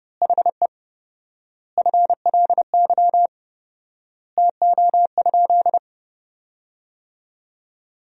Morse code, heard as a keyed tone, about 29 wpm, 700 Hz. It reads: HE FLY TO?
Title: 4x Word-Spacing - Sentences from Top 500 Words - 30wpm